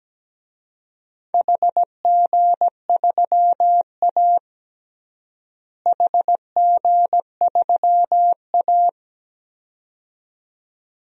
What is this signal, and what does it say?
Morse code, heard as a keyed tone, about 17 wpm, 700 Hz. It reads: HG3A HG3A
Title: DXCC-100 Call Signs - 4 Characters - Repeated 2x - 17wpm